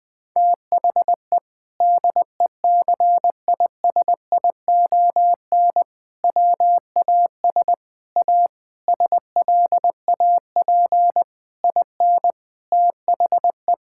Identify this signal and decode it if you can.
Morse code, heard as a keyed tone, about 21 wpm, 700 Hz. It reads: THE DECISION WAS A SLAP IN THE
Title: Sentences with 6-Word Phrases - Spoken Beginning - 20wpm